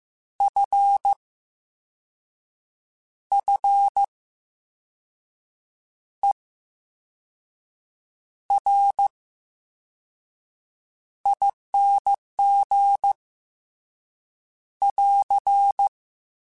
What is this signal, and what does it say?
Morse code, heard as a keyed tone, about 15 wpm, spaced as for 5 wpm, 800 Hz. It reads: FFERING <AR>
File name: Special INGs / Special INGs-mp3s / Slow ing @13